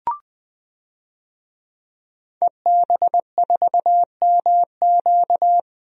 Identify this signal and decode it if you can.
Morse code, heard as a keyed tone, about 21 wpm, 700 Hz. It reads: EB4MQ
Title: DXCC-100 Call Signs - 5 Characters - Repeated 3x - 20wpm